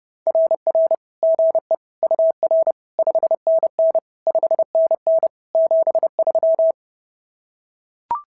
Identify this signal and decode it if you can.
Morse code, heard as a keyed tone, about 26 wpm, 650 Hz. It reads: RR GE UR 5NN 5NN 73
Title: SOTA QSO Practice 30wpm